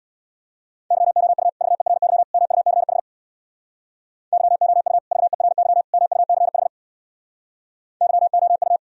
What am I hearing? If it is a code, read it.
Morse code, heard as a keyed tone, about 64 wpm, 700 Hz. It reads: YOU VERY MUCH YOU VERY MUCH YOU